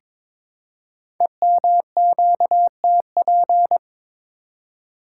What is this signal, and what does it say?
Morse code, heard as a keyed tone, about 22 wpm, 700 Hz. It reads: EMQTP
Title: Code Groups - 5 Letters - 22wpm